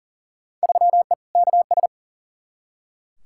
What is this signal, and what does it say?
Morse code, heard as a keyed tone, about 40 wpm, 700 Hz. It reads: 3E KS
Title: Field Day Exchanges Encore - 40wpm